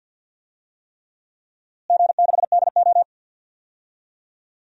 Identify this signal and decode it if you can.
Morse code, heard as a keyed tone, about 52 wpm, 700 Hz. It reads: G6DO
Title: DXCC-100 Call Signs - 4 Characters - Repeated 3x - 50wpm